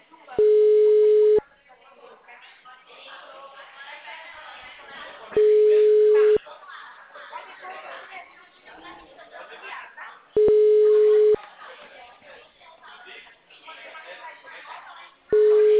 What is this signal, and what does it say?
Ringback tone (Europe, 425 Hz), with speech.
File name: out- -2003- - - .36288